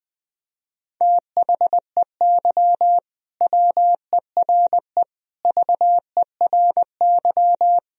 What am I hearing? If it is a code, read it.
Morse code, heard as a keyed tone, about 21 wpm, 700 Hz. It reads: THEY WERE VERY
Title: Sentences from Top 1000 Words - Review Part 3 - 20wpm